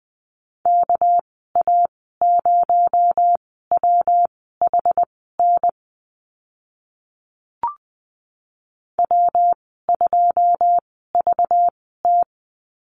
Morse code, heard as a keyed tone, about 20 wpm, spaced as for 14 wpm, 700 Hz. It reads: KA0WHN W2VT
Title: 219 Lesson 35g - Number 0 - Call Signs 20wpm